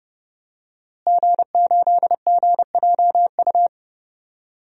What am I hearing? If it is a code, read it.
Morse code, heard as a keyed tone, about 30 wpm, 700 Hz. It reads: G8GJU
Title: DXCC-100 Call Signs - 5 Characters - Repeated 2x - 30wpm